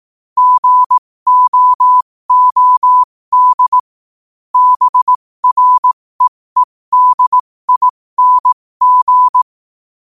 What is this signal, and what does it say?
Morse code, heard as a keyed tone, about 18 wpm, spaced as for 15 wpm, 1000 Hz. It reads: GOOD BREEDING
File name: qotdfiles / 15wpm